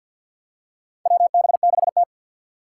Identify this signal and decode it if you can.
Morse code, heard as a keyed tone, about 51 wpm, 700 Hz. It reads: WB6T
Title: USA Call Signs - 4 Characters - 6x Character Spacing - 50wpm